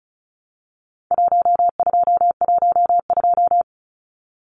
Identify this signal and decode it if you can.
Morse code, heard as a keyed tone, about 34 wpm, 700 Hz. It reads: 1212
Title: Numbers - Contest Style - 4-Digit 35wpm